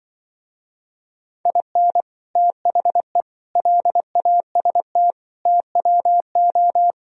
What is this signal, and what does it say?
Morse code, heard as a keyed tone, about 24 wpm, 700 Hz. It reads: IN THE LAST TWO
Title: Speed-Racing - Sets of 5 Words 20wpm